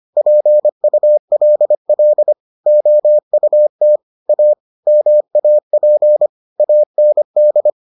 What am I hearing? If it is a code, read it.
Morse code, heard as a keyed tone, about 25 wpm, 600 Hz. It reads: PULL OUT A MAP AND